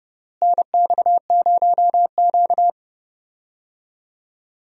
Morse code, heard as a keyed tone, about 30 wpm, 700 Hz. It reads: NX0Q
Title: USA Call Signs - 4 Characters - Repeated 3x - 30wpm